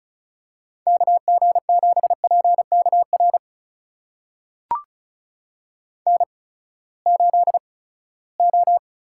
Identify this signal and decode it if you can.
Morse code, heard as a keyed tone, about 36 wpm, spaced as for 20 wpm, 700 Hz. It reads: KG7PKR N8O
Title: USA Call Signs - 8x Character Spacing - 35wpm